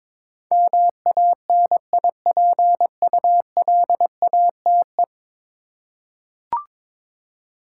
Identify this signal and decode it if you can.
Morse code, heard as a keyed tone, about 22 wpm, 700 Hz. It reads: MANIPULATE
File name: practice / 22wpm / 10 Letter Words - First 3 Letters Given - 22wpm